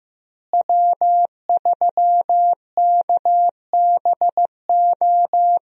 Morse code, heard as a keyed tone, about 15 wpm, 700 Hz. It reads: W3KBO
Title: USA Call Signs - 5 Characters - Repeated 3x - 15wpm